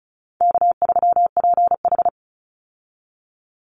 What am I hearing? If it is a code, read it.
Morse code, heard as a keyed tone, about 34 wpm, 700 Hz. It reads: K3PH
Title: Call Signs - 35wpm